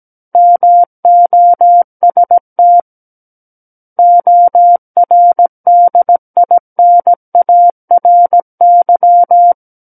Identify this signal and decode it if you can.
Morse code, heard as a keyed tone, about 17 wpm, 700 Hz. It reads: MOST ORDINARY
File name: Book One / 08-Chapter 8 - 17 WPM